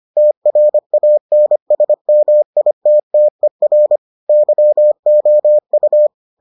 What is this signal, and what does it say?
Morse code, heard as a keyed tone, about 25 wpm, 600 Hz. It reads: TRANSMITTER YOU